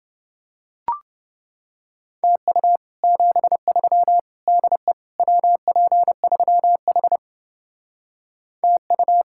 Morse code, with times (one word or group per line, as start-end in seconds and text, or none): 2.2-2.8 TU
3.0-4.2 73
4.5-4.9 DE
5.2-7.2 WP3H
8.6-9.2 TU